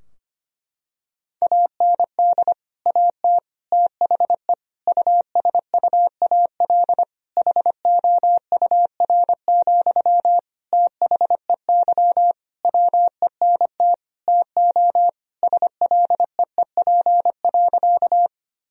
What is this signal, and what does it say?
Morse code, heard as a keyed tone, about 25 wpm, 700 Hz. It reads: AND AT THE USUAL HOUR, THEY WENT TO SLEEP.
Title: Sentences from Top 1000 Words - Review Part 2 - 25wpm